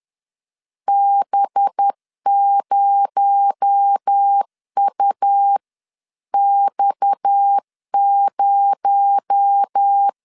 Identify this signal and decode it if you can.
Morse code, heard as a keyed tone, about 10 wpm, 800 Hz. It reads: B0UX0